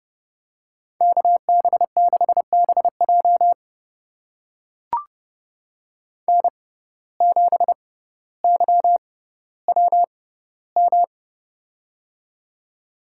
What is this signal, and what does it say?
Morse code, heard as a keyed tone, about 30 wpm, spaced as for 15 wpm, 700 Hz. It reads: KB6BJ N7YWM
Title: USA Call Signs - 6x Character Spacing - 30wpm